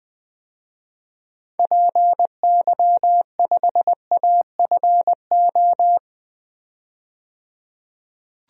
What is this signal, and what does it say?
Morse code, heard as a keyed tone, about 20 wpm, 700 Hz. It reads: PY5AFO
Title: DXCC-100 Call Signs - 6 Characters - 20wpm